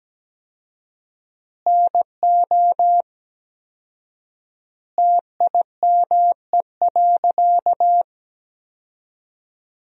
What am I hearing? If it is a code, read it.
Morse code, heard as a keyed tone, about 17 wpm, 700 Hz. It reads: NO TIME.
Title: 4x Word-Spacing - Sentences from Top 100 Words - 17wpm